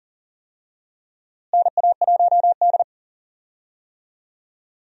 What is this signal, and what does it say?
Morse code, heard as a keyed tone, about 40 wpm, 700 Hz. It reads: NA1D
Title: USA Call Signs - 4 Characters - Repeated 3x - 40wpm